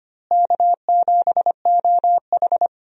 Morse code, heard as a keyed tone, about 25 wpm, 700 Hz. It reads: K7OH
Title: USA Call Signs - 4 Characters - Repeated 3x - 25wpm